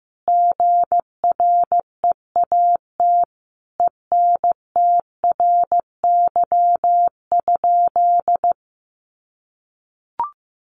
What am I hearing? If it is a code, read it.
Morse code, heard as a keyed tone, about 14 wpm, 700 Hz. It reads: GREAT ENTRY?
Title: Sentences from Top 1500 Words 15wpm